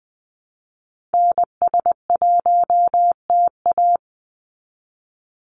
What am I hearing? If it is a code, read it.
Morse code, heard as a keyed tone, about 20 wpm, 700 Hz. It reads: NS1TA
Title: NA SOTA Activator Callsigns 20wpm